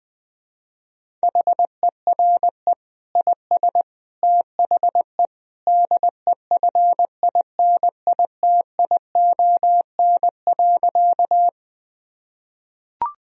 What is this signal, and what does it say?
Morse code, heard as a keyed tone, about 20 wpm, 700 Hz. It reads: HERE IS THE DEFINITION.
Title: Sentences from Top 1000 Words - Review Part 4 - 20wpm